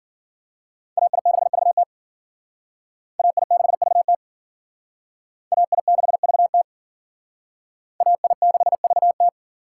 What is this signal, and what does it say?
Morse code, heard as a keyed tone, about 52 wpm, 700 Hz. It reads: AI6VT AI6VT AI6VT AI6VT